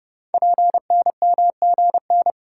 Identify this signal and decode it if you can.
Morse code, heard as a keyed tone, about 30 wpm, 690 Hz. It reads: PNMGN
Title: Code Groups - 5 Letters - 30wpm